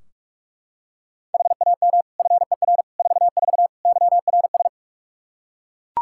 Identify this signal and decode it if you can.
Morse code, heard as a keyed tone, about 41 wpm, 700 Hz. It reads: HAM FER 44 YRS
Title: Practice QSO - Exchange 3 - More Info - 45wpm